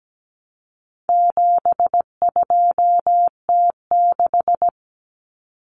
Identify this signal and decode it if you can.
Morse code, heard as a keyed tone, about 17 wpm, 700 Hz. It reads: 72T6
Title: Common Cut Numbers - 4-Digit 17wpm